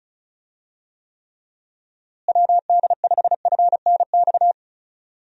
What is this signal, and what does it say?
Morse code, heard as a keyed tone, about 34 wpm, 700 Hz. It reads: WD5FNX
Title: USA Call Signs - 6 Characters - Repeated 2x - 35wpm